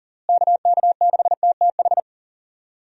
Morse code, heard as a keyed tone, about 40 wpm, 700 Hz. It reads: KK6TTH
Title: USA Call Signs - 6 Characters - Repeated 2x - 40wpm